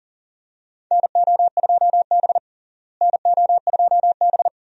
Morse code, heard as a keyed tone, about 40 wpm, 700 Hz. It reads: NO2B NO2B
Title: USA Call Signs - 4 Characters - Repeated 3x - 40wpm